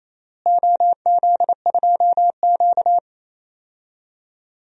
Morse code, heard as a keyed tone, about 28 wpm, 700 Hz. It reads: OZ2Q
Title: DXCC-100 Call Signs - 4 Characters - Repeated 3x - 28wpm